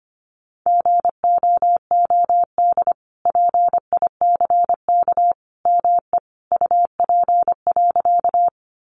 Morse code, heard as a keyed tone, about 25 wpm, 700 Hz. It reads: GOOD PICK ME UP.
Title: Sentences from Top 1000 Words 25wpm